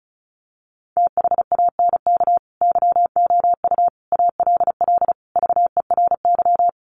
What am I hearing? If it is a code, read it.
Morse code, heard as a keyed tone, about 34 wpm, 700 Hz. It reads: THANK YOU ALL VERY
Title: Sets of 5 Words Encore 35wpm